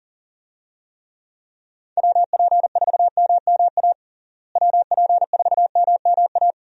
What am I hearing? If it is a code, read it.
Morse code, heard as a keyed tone, about 37 wpm, 700 Hz. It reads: WP4MMA WP4MMA
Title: USA Call Signs - 6 Characters - Repeated 2x - 40wpm